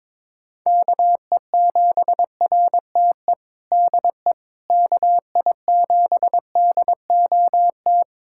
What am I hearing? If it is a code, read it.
Morse code, heard as a keyed tone, about 22 wpm, 700 Hz. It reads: KE7RTE DE KI7DOT
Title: USA Callsign DE Callsign - 6 Characters - 22wpm